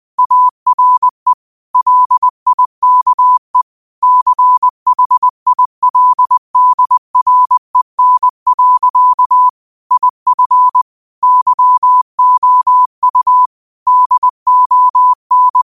Morse code, heard as a keyed tone, about 20 wpm, 1000 Hz. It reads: ARE LIKE CHILDREN. IF YOU DON
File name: qotdfiles / 20wpm